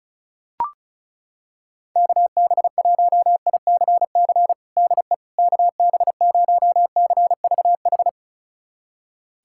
Morse code, heard as a keyed tone, about 36 wpm, 700 Hz. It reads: KB1ICC DE KB0CVH
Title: USA Callsign DE Callsign 35wpm